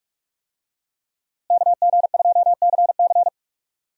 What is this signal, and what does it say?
Morse code, heard as a keyed tone, about 46 wpm, 700 Hz. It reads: KG2CC
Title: USA Call Signs - 5 Characters - Repeated 2x - 45wpm